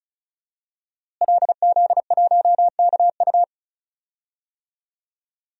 Morse code, heard as a keyed tone, about 36 wpm, 700 Hz. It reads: LZ1KU
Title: DXCC-100 Call Signs - 5 Characters - Repeated 3x - 35wpm